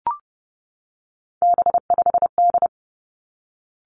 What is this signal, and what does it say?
Morse code, heard as a keyed tone, about 30 wpm, 700 Hz. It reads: B5D
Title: Three Challenging Letter-Number Combinations 30wpm